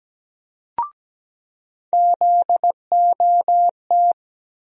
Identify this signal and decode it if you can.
Morse code, heard as a keyed tone, about 17 wpm, 700 Hz. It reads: ZOT
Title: USA Call Sign Postfixes - 3 Letters 17wpm